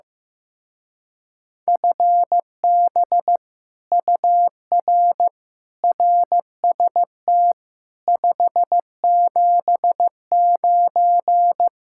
Morse code, heard as a keyed tone, about 15 wpm, 700 Hz. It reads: FB UR RST 579
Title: Speed-Racing Common QSO Phrases Encore - 15wpm